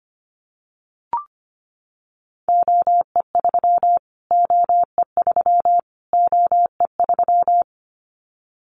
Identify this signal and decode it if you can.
Morse code, heard as a keyed tone, about 25 wpm, 700 Hz. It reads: OE3 OE3 OE3
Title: Callsign Prefixes for Top 25 DXCC - Repeated 3x - 25wpm